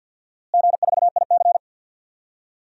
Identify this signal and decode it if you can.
Morse code, heard as a keyed tone, about 53 wpm, 700 Hz. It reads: G4IC
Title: DXCC-100 Call Signs - 4 Characters - Repeated 3x - 50wpm